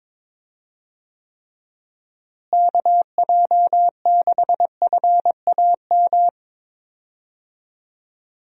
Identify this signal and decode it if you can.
Morse code, heard as a keyed tone, about 22 wpm, 700 Hz. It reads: KJ6FAM
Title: USA Call Signs - 6 Characters - Repeated 2x - 22wpm